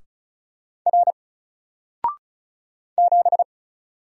Morse code, heard as a keyed tone, about 36 wpm, 700 Hz. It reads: R 7
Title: Single Character (Letter:Number:Punctuation:Prosign) 35wpm